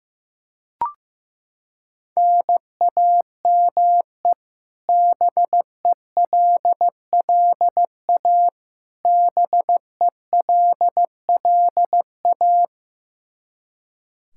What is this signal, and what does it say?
Morse code, heard as a keyed tone, about 15 wpm, 700 Hz. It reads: NAME BELLA BELLA
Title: Practice QSO - Exchange 1 - Beginning - 15wpm